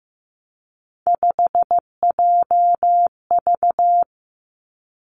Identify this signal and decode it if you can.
Morse code, heard as a keyed tone, about 15 wpm, 700 Hz. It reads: 5JV
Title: Three Challenging Letter-Number Combinations 15wpm